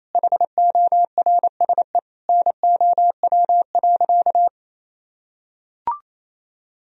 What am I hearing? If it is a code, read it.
Morse code, heard as a keyed tone, about 28 wpm, 700 Hz. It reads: HORSE NOW.
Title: Sentences from Top 1000 Words - Review Part 1 - 28wpm